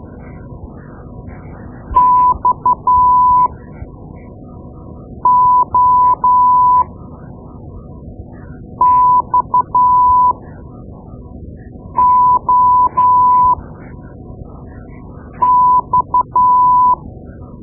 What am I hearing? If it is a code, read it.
Morse code, heard as a keyed tone, about 12 wpm, spaced as for 4 wpm, 1000 Hz. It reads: XOXOX
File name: Morse Lessons / Morse Lessons / Tape-2B